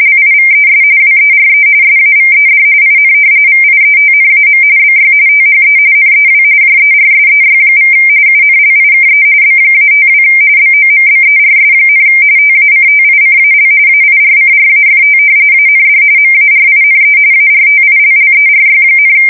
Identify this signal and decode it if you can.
RTTY, about 45.45 baud. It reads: LIVES WERE SAVED IN VIETNAM BY THE NEW JERSEY= UNLIKE B-52 BOMBERS, TACTICAL AIRCRAFT AND MARINE ARTILLERY,
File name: fsk170 / battleshipnj